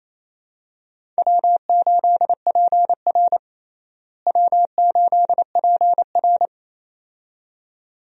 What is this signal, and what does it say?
Morse code, heard as a keyed tone, about 28 wpm, 700 Hz. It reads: W8PR W8PR
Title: USA Call Signs - 4 Characters - Repeated 2x - 28wpm